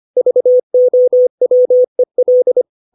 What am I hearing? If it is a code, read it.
Morse code, heard as a keyed tone, about 25 wpm, 500 Hz. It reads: VOWEL
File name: 632 words / 632 Words 25WPM